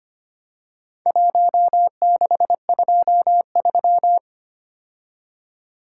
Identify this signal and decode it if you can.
Morse code, heard as a keyed tone, about 25 wpm, 700 Hz. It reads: 1623
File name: practice / 25wpm / Four Letter-Number Combinations - 25wpm